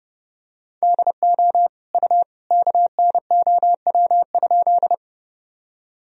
Morse code, heard as a keyed tone, about 30 wpm, 700 Hz. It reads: DO U KNOW?
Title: Sentences from Top 100 Words and Abbreviations 30wpm